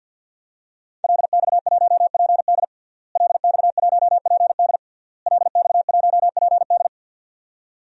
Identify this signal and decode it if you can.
Morse code, heard as a keyed tone, about 52 wpm, 700 Hz. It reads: LX1PD LX1PD LX1PD
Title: DXCC-100 Call Signs - 5 Characters - Repeated 3x - 50wpm